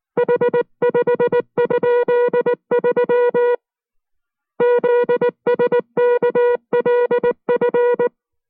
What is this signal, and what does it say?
Morse code, heard as a keyed tone, about 19 wpm, 480 Hz. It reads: H5?3 ZSKLF